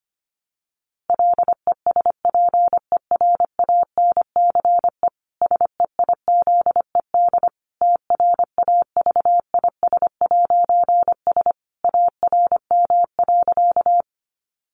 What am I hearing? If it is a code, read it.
Morse code, heard as a keyed tone, about 25 wpm, 700 Hz. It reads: LESPERANCE SEIZED TRAVIS'S ARM.
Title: A Sound Of Thunder 25wpm